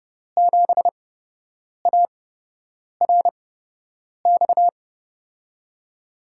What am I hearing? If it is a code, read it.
Morse code, heard as a keyed tone, about 29 wpm, spaced as for 8 wpm, 700 Hz. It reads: 7ARX
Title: USA Call Signs - 6 Characters - 8x Character Spacing - 30wpm